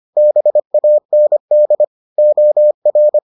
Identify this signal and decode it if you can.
Morse code, heard as a keyed tone, about 25 wpm, 600 Hz. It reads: BAND OR